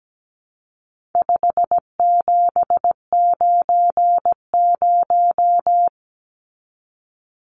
Morse code, heard as a keyed tone, about 17 wpm, 700 Hz. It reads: 5790
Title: Numbers - 4-Digit 17wpm